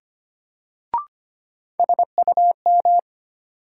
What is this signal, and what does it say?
Morse code, heard as a keyed tone, about 25 wpm, 700 Hz. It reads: SUM
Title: Sentences from Top 1000 Words - Review Part 6 - New Word Drill - 25wpm